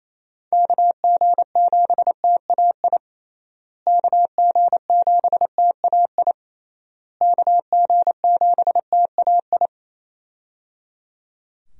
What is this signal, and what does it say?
Morse code, heard as a keyed tone, about 27 wpm, 700 Hz. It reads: KG7TAI KG7TAI KG7TAI
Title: USA Call Signs - 6 Characters - Repeated 3x - 28wpm